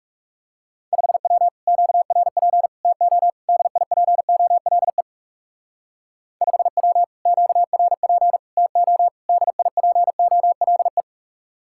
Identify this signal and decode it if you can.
Morse code, heard as a keyed tone, about 34 wpm, 700 Hz. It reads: 5W QRP TO DIPOLE 5W QRP TO DIPOLE
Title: Speed-Racing Common QSO Phrases - 30wpm